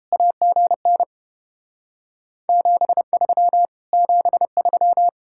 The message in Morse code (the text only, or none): AGN 73 73